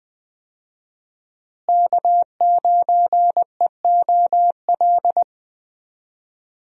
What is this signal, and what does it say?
Morse code, heard as a keyed tone, about 20 wpm, 700 Hz. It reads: K9EOL
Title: USA Call Signs - 5 Characters - Repeated 2x - 20wpm